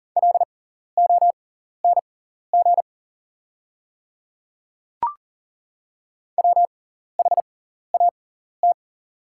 Morse code, heard as a keyed tone, about 40 wpm, spaced as for 14 wpm, 700 Hz. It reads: LONG WHAT
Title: 6x Character Spacing - Top 100 Words - 40wpm